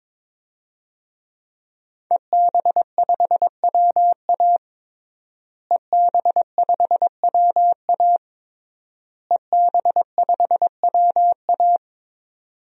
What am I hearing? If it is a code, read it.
Morse code, heard as a keyed tone, about 21 wpm, 700 Hz. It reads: EB5WA EB5WA EB5WA
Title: DXCC-100 Call Signs - 5 Characters - Repeated 3x - 22wpm